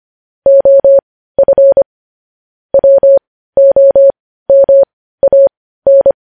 Morse code, heard as a keyed tone, about 25 wpm, spaced as for 15 wpm, 550 Hz. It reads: OF WOMAN
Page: news-headlines-in-morse-code-at-15-wpm